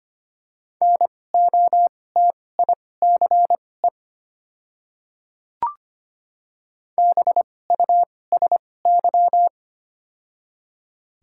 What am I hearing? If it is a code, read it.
Morse code, heard as a keyed tone, about 25 wpm, spaced as for 18 wpm, 700 Hz. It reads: NOTICE BUSY
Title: 2x Character Spacing - Top 500 Words - 25wpm